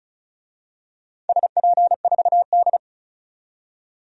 Morse code, heard as a keyed tone, about 36 wpm, 700 Hz. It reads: SP4D
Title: DXCC-100 Call Signs - 4 Characters - 35wpm